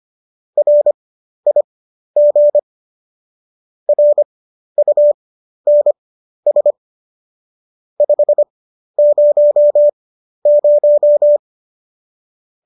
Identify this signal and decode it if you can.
Morse code, heard as a keyed tone, about 25 wpm, spaced as for 12 wpm, 590 Hz. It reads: RIG RUNS 500